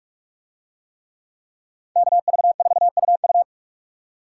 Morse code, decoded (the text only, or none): KV4UU